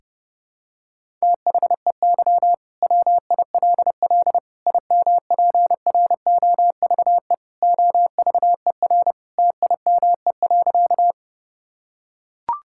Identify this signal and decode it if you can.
Morse code, heard as a keyed tone, about 32 wpm, 700 Hz. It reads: THEY WILL IMPROVE OVER TIME.
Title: Sentences from Top 1000 Words - Review Part 5 - 30wpm